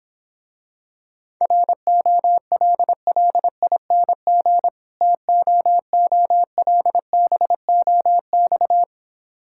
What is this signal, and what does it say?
Morse code, heard as a keyed tone, about 26 wpm, 700 Hz. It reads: ROLLING TOOLBOX